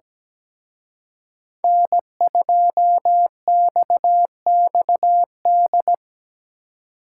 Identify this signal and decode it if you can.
Morse code, heard as a keyed tone, about 20 wpm, 700 Hz. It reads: N2XXD
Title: Speed Racing - USA 1x3 Call Signs 17wpm